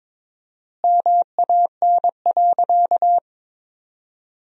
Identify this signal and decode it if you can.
Morse code, heard as a keyed tone, about 23 wpm, 700 Hz. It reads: MAN.
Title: 4x Word-Spacing - Sentences from Top 500 Words - 22wpm